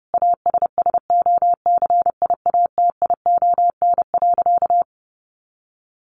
Morse code, heard as a keyed tone, about 30 wpm, 700 Hz. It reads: ASSOCIATION.
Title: Sentences from Top 1200 Words 30wpm